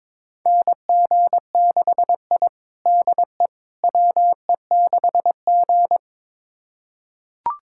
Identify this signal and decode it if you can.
Morse code, heard as a keyed tone, about 21 wpm, 700 Hz. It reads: NG6I DE WE6G
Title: USA Callsign DE Callsign - 4 Characters - 22wpm